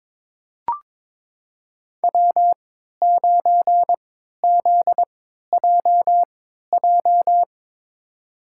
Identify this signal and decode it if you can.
Morse code, heard as a keyed tone, about 22 wpm, spaced as for 13 wpm, 700 Hz. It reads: W9ZJJ
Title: USA Call Signs - 5 Characters - 3x Character Spacing - 22wpm